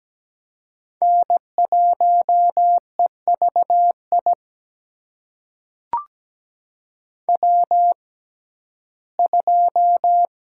Morse code, heard as a keyed tone, about 17 wpm, spaced as for 12 wpm, 700 Hz. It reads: N1EVI W2
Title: USA Call Signs - 5 Characters - 6x Character Spacing - 17wpm